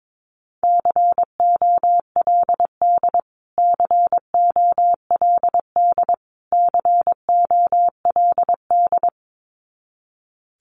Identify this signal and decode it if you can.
Morse code, heard as a keyed tone, about 22 wpm, 700 Hz. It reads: COLD COLD COLD
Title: Top 500 Words Repeated 22wpm